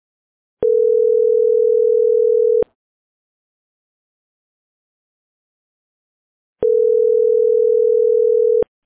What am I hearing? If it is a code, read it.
Ringback tone (North America), 440 + 480 Hz.